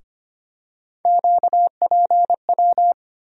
Morse code, heard as a keyed tone, about 25 wpm, 700 Hz. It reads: QPW
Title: USA Call Sign Postfixes - 3 Letters 25wpm